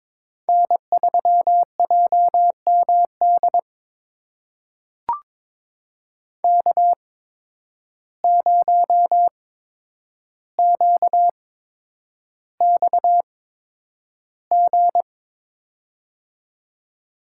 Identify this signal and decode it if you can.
Morse code, heard as a keyed tone, about 22 wpm, spaced as for 9 wpm, 700 Hz. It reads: N3JMD K0QXG
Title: USA Call Signs - 5 Characters - 8x Character Spacing - 22wpm